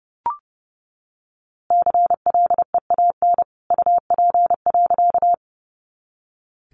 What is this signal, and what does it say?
Morse code, heard as a keyed tone, about 30 wpm, 700 Hz. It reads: CLEAN UP.
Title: A Sound Of Thunder 30wpm